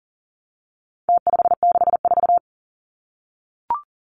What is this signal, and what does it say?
Morse code, heard as a keyed tone, about 40 wpm, 700 Hz. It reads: T564
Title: Numbers - Contest Style - 4-Digit 40wpm